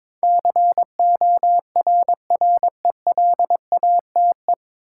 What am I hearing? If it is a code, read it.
Morse code, heard as a keyed tone, about 21 wpm, 700 Hz. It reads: CORRELATE
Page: Morse Code Ninja practice 22wpm